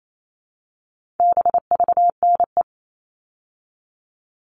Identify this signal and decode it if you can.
Morse code, heard as a keyed tone, about 28 wpm, 700 Hz. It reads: BVNE
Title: All Cut Numbers - 4-Digit 28wpm